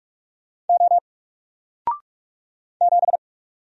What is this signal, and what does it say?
Morse code, heard as a keyed tone, about 44 wpm, 700 Hz. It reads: O 7
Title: Single Letter-Number 45wpm